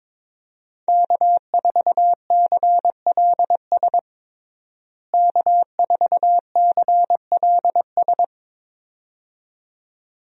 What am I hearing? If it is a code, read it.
Morse code, heard as a keyed tone, about 22 wpm, 700 Hz. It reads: K4CLS K4CLS
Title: USA Call Signs - 5 Characters - Repeated 2x - 22wpm